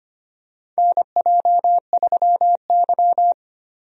Morse code, heard as a keyed tone, about 25 wpm, 700 Hz. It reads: NJ3Y
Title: USA Call Signs - 4 Characters - Repeated 3x - 25wpm